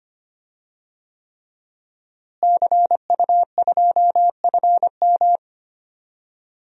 Morse code, heard as a keyed tone, about 25 wpm, 700 Hz. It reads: CU2FM
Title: DXCC-100 Call Signs - 5 Characters - Repeated 3x - 25wpm